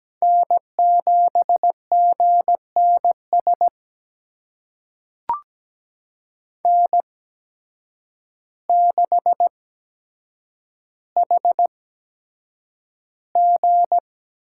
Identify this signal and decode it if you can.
Morse code, heard as a keyed tone, about 17 wpm, spaced as for 8 wpm, 700 Hz. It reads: N7GNS N6HG
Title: USA Call Signs - 5 Characters - 8x Character Spacing - 17wpm